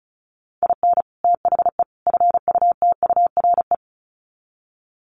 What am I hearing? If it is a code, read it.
Morse code, heard as a keyed tone, about 34 wpm, 700 Hz. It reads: IN THE FUTURE
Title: Sets of 3 Words 35wpm